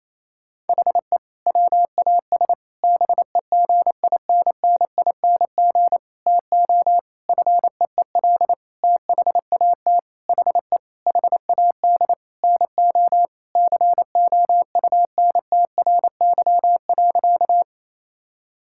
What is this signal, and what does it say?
Morse code, heard as a keyed tone, about 27 wpm, 700 Hz. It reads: HE WAS BEGINNING TO FEEL THAT HE HAD NO COUNTRY.
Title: Sentences from Top 1000 Words - Review Part 5 - 28wpm